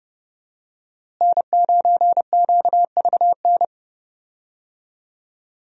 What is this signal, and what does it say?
Morse code, heard as a keyed tone, about 30 wpm, 700 Hz. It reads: N9QVN
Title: USA Call Signs - 5 Characters - Repeated 2x - 30wpm